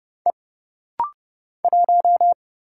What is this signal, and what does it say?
Morse code, heard as a keyed tone, about 30 wpm, 700 Hz. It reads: E 1
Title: Single Letter-Number Alt - Rapid-Fire 30wpm